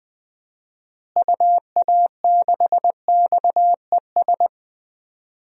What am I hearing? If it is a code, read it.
Morse code, heard as a keyed tone, about 20 wpm, 700 Hz. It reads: UA6XES
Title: DXCC-100 Call Signs - 6 Characters - Repeated 2x - 20wpm